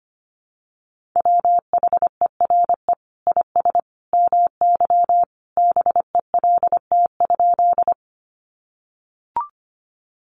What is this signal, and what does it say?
Morse code, heard as a keyed tone, about 25 wpm, 700 Hz. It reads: WHERE IS MY BELT?
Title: Sentences from Top 1400 Words 25wpm